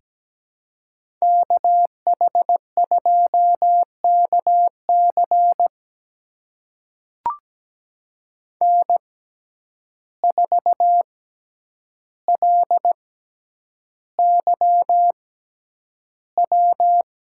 Morse code, heard as a keyed tone, about 17 wpm, spaced as for 9 wpm, 700 Hz. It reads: KH2KC N4LYW
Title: USA Call Signs - 5 Characters - 6x Character Spacing - 17wpm